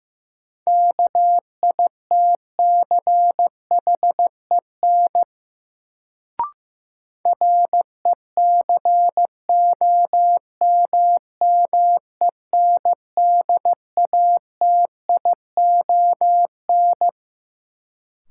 Morse code, heard as a keyed tone, about 16 wpm, 700 Hz. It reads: KITCHEN RECOMMENDATION
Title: Sentences from Top 1400 Words - New Word Drill - 15wpm